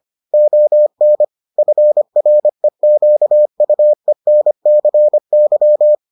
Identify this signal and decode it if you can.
Morse code, heard as a keyed tone, about 25 wpm, 600 Hz. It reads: ON FREQUENCY